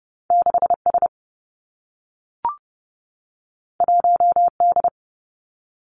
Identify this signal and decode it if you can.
Morse code, heard as a keyed tone, about 30 wpm, 700 Hz. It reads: 6 S 1 D
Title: Two Challenging Letter-Number Combinations 30wpm